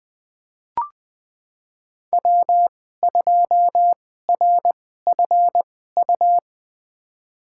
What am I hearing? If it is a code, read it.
Morse code, heard as a keyed tone, about 20 wpm, spaced as for 14 wpm, 700 Hz. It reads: W2RFU
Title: USA Call Signs - 5 Characters - 2x Character Spacing - 20wpm